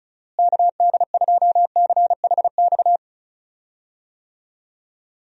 Morse code, heard as a keyed tone, about 36 wpm, 700 Hz. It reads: KD2CHX
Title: USA Call Signs - 6 Characters - Repeated 2x - 35wpm